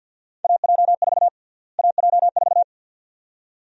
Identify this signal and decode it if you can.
Morse code, heard as a keyed tone, about 51 wpm, 700 Hz. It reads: AJ4 AJ4
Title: USA Call Sign Prefixes - Repeated 2x - 50wpm